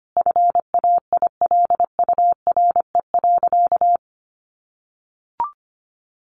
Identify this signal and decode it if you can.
Morse code, heard as a keyed tone, about 25 wpm, 700 Hz. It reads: FAILURE.
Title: Sentences from Top 900 Words 25wpm